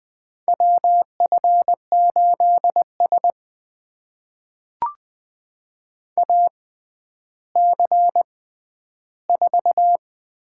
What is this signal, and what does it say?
Morse code, heard as a keyed tone, about 20 wpm, spaced as for 11 wpm, 700 Hz. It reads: WF8S AC4
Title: USA Call Signs - 6x Character Spacing - 20wpm